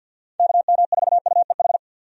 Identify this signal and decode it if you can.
Morse code, heard as a keyed tone, about 50 wpm, 700 Hz. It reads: KM4UEH
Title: USA Call Signs - 6 Characters - 50wpm